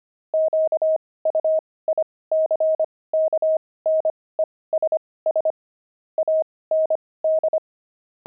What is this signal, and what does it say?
Morse code, heard as a keyed tone, about 25 wpm, spaced as for 18 wpm, 630 Hz. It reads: QUICKNESS AND
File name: pangram-fs-18-cs-25-dexterity